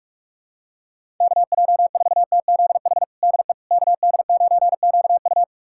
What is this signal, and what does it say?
Morse code, heard as a keyed tone, about 46 wpm, 700 Hz. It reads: KJ4TZH DE KD9QU
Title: USA Callsign DE Callsign 45wpm